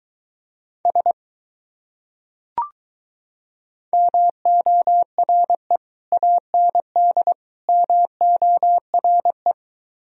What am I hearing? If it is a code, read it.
Morse code, heard as a keyed tone, about 23 wpm, 700 Hz. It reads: S MORE AND MORE